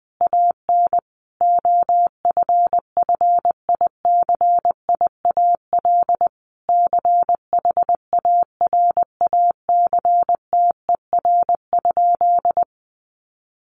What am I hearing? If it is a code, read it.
Morse code, heard as a keyed tone, about 19 wpm, 700 Hz. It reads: AN OFFICIAL CHARACTER?
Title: Sentences from Top 1200 Words 20wpm